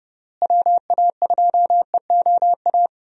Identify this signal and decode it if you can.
Morse code, heard as a keyed tone, about 30 wpm, 700 Hz. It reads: WA2EOA
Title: USA Call Signs - 6 Characters - Repeated 3x - 30wpm